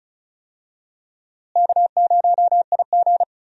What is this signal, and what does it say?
Morse code, heard as a keyed tone, about 34 wpm, 700 Hz. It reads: K0IG
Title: USA Call Signs - 4 Characters - Repeated 2x - 35wpm